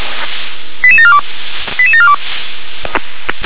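Five-tone selective call (ZVEI), heard as quick runs of five tones, tones of about 70 ms. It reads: 88742 88742
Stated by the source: Received by SDR, ZVEI five-tone call 88747.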